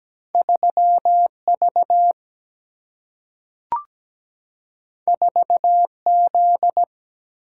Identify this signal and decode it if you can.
Morse code, heard as a keyed tone, about 17 wpm, 700 Hz. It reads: 3 V 4 Z
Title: Single-Letter Single-Number 17wpm